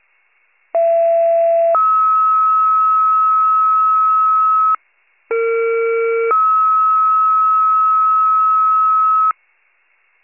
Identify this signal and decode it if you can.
Two-tone page, heard as a one-second tone, then a three-second tone, about 670 then 1290 Hz, about 460 then 1290 Hz. It is heard from 0.7 to 9.3 s.